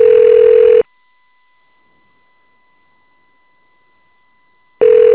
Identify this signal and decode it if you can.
Ringback tone (North America), 440 + 480 Hz.